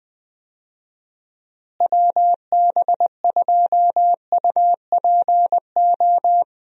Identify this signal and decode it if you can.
Morse code, heard as a keyed tone, about 20 wpm, 700 Hz. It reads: WB2UPO
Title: USA Call Signs - 6 Characters - Repeated 3x - 20wpm